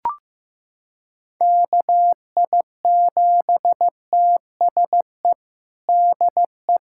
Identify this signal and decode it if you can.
Morse code, heard as a keyed tone, about 16 wpm, 700 Hz. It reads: KI7TSE DE
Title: USA Callsign DE Callsign 15wpm